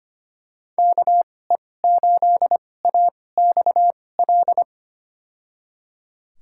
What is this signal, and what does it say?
Morse code, heard as a keyed tone, about 25 wpm, spaced as for 18 wpm, 700 Hz. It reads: KE8AXL
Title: USA Call Signs - 6 Characters - 2x Character Spacing - 25wpm